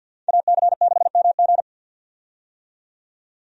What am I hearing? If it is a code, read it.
Morse code, heard as a keyed tone, about 51 wpm, 700 Hz. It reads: AC6MG
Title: USA Call Signs - 5 Characters - 50wpm